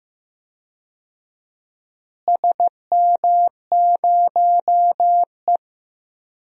Morse code, heard as a keyed tone, about 15 wpm, 700 Hz. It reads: SM0E
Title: DXCC-100 Call Signs - 4 Characters - Repeated 3x - 15wpm